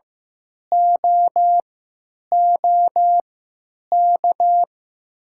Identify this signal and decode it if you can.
Morse code, heard as a keyed tone, about 15 wpm, spaced as for 9 wpm, 700 Hz. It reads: OOK
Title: 3x Character Spacing - Top 300 Words - 15wpm